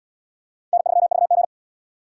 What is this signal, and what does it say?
Morse code, heard as a keyed tone, about 71 wpm, 700 Hz. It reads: N3VG